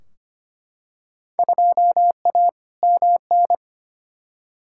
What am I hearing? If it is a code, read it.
Morse code, heard as a keyed tone, about 25 wpm, 700 Hz. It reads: 2A MN
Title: Field Day Exchanges Encore - 25wpm